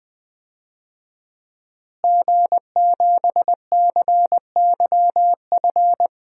Morse code, heard as a keyed tone, about 20 wpm, 700 Hz. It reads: G7CYF
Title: DXCC-100 Call Signs - 5 Characters - Repeated 2x - 20wpm